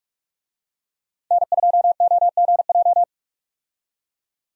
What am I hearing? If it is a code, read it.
Morse code, heard as a keyed tone, about 43 wpm, 700 Hz. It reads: N2OGJ
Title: USA Call Signs - 5 Characters - 45wpm